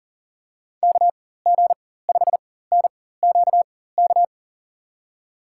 Morse code, heard as a keyed tone, about 40 wpm, spaced as for 19 wpm, 700 Hz. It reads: KG5NQK
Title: USA Call Signs - 6 Characters - 4x Character Spacing - 40wpm